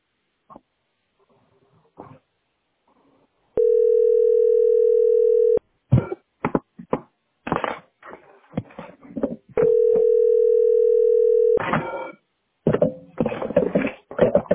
Ringback tone (North America), 440 + 480 Hz.